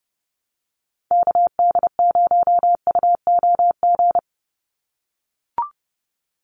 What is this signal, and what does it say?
Morse code, heard as a keyed tone, about 30 wpm, 700 Hz. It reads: KD0UOG
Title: USA Call Signs 30wpm